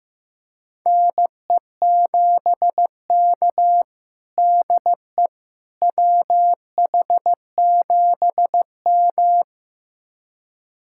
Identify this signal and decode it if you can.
Morse code, heard as a keyed tone, about 16 wpm, 700 Hz. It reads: NE7K DE WH7M
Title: USA Callsign DE Callsign - 4 Characters - 15wpm